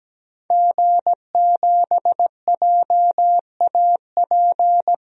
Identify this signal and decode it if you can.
Morse code, heard as a keyed tone, about 17 wpm, 700 Hz. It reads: G7JAP
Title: DXCC-100 Call Signs - 5 Characters - 17wpm